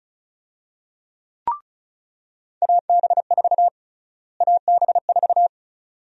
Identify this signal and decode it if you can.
Morse code, heard as a keyed tone, about 36 wpm, 700 Hz. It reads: AB4 AB4
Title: USA Call Sign Prefixes - Repeated 2x - 35wpm